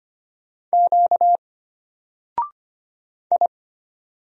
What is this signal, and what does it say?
Morse code, heard as a keyed tone, about 25 wpm, 700 Hz. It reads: Q I
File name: Single-Letter-Number-25wpm